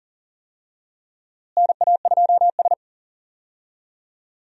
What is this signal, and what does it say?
Morse code, heard as a keyed tone, about 40 wpm, 700 Hz. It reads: NA2S
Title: USA Call Signs - 4 Characters - Repeated 3x - 40wpm